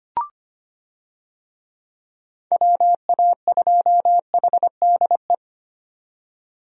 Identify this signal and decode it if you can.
Morse code, heard as a keyed tone, about 25 wpm, 700 Hz. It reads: WA2HDE
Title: USA Call Signs - 6 Characters - 25wpm